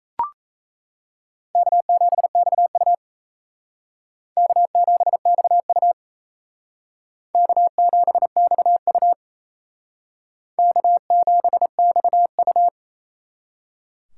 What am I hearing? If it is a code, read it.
Morse code, heard as a keyed tone, about 35 wpm, 700 Hz. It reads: K7XU K7XU K7XU K7XU
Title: Speed Racing - USA 1x2 Call Signs 28wpm